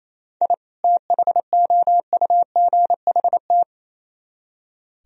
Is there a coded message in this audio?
Yes: Morse code from 0.4 to 3.6 s, heard as a keyed tone, about 36 wpm, 700 Hz. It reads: I THOUGHT